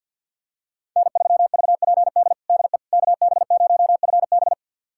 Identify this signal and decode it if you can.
Morse code, heard as a keyed tone, about 50 wpm, 700 Hz. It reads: N3VPD DE KB0FB
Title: USA Callsign DE Callsign - 5 Characters - 50wpm